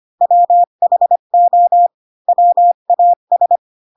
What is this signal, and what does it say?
Morse code, heard as a keyed tone, about 25 wpm, 700 Hz. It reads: WHO WAS
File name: Pinocchio / Pinocchio ch4 20 25 ik1hge